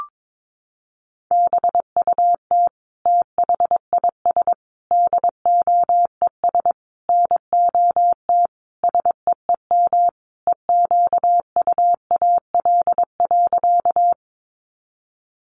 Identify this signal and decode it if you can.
Morse code, heard as a keyed tone, about 22 wpm, 700 Hz. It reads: BUT THIS DOES NOT SEEM EQUAL.
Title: Sentences from Top 1400 Words 22wpm